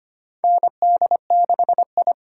Morse code, heard as a keyed tone, about 25 wpm, 700 Hz. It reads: ND6I
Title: USA Call Signs - 8x Character Spacing - 25wpm